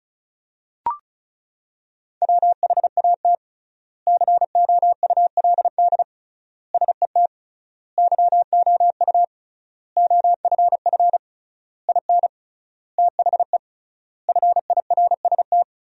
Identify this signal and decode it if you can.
Morse code, heard as a keyed tone, about 34 wpm, 700 Hz. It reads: WHAT COULD SET YOU OFF IN THE FIRST
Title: 3x Word-Spacing - Sentences from Top 200 Words - 35wpm